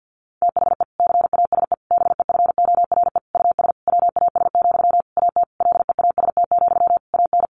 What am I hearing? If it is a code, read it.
Morse code, heard as a keyed tone, about 50 wpm, 700 Hz. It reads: THE CASE BEFORE US WAS, AT LEAST, AN
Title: Sentences from Top 1400 Words 50wpm